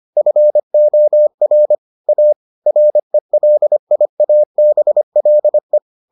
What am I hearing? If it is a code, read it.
Morse code, heard as a keyed tone, about 25 wpm, 600 Hz. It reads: FOR A RELIABLE